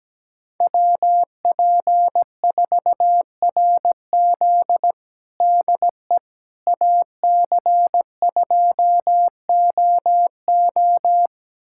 Morse code, heard as a keyed tone, about 17 wpm, 700 Hz. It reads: WP4RZ DE AC2OO
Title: USA Callsign DE Callsign - 5 Characters - 17wpm